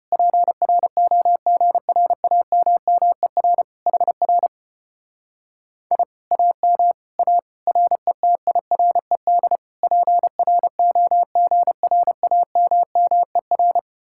Morse code, heard as a keyed tone, about 33 wpm, 700 Hz. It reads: PROGRAMMER HR I AM A RETIRED PROGRAMMER